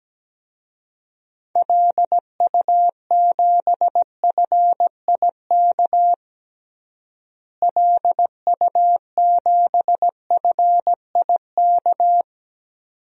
Morse code, heard as a keyed tone, about 17 wpm, 700 Hz. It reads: LU7FIK LU7FIK
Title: DXCC-100 Call Signs - 6 Characters - Repeated 2x - 17wpm